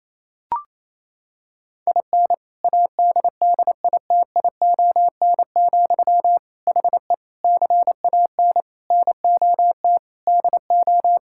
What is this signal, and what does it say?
Morse code, heard as a keyed tone, about 28 wpm, 700 Hz. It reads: IN ADDITION, HE CAN NOT DO
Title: Sentences from Top 1000 Words - Review Part 6 - 28wpm